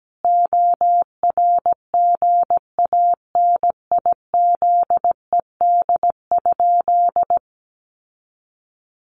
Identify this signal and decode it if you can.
Morse code, heard as a keyed tone, about 17 wpm, 700 Hz. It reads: ORGANIZED?
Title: Sentences from Top 1300 Words 17wpm